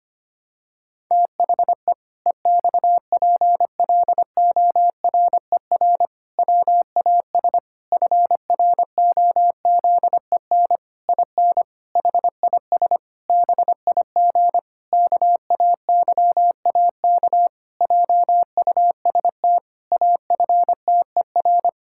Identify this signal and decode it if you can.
Morse code, heard as a keyed tone, about 25 wpm, 700 Hz. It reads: THE EXPLORER WAS FROZEN IN HIS BIG KAYAK JUST AFTER
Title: Pangram Challenge v2 - 25wpm